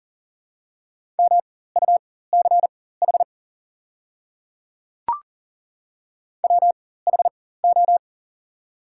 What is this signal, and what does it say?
Morse code, heard as a keyed tone, about 40 wpm, spaced as for 19 wpm, 700 Hz. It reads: MUCH WHO
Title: 4x Character Spacing - Top 300 Words - 40wpm